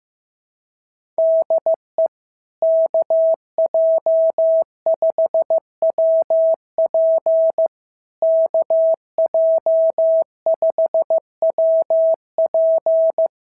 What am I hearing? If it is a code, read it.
Morse code, heard as a keyed tone, about 14 wpm, 650 Hz. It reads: DE KJ5WP KJ5WP
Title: SOTA QSO Practice 15wpm